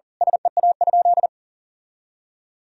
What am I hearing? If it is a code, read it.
Morse code, heard as a keyed tone, about 40 wpm, 700 Hz. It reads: SEA?
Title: Sentences from Top 1000 Words - Review Part 1 - 40wpm